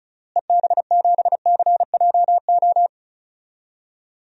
Morse code, heard as a keyed tone, about 34 wpm, 700 Hz. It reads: EB7CJO
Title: DXCC-100 Call Signs - 6 Characters - Repeated 3x - 35wpm